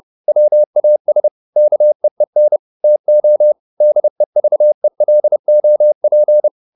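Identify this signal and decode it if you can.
Morse code, heard as a keyed tone, about 30 wpm, 600 Hz. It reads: WAS KEEN TO DEVELOP
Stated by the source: Morse code at 30 wpm, the novel Never by Ken Follett in English, from CW med Gnister.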